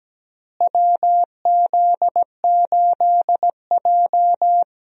Morse code, heard as a keyed tone, about 17 wpm, 700 Hz. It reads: WZ8J
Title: USA Call Signs - 4 Characters - Repeated 2x - 17wpm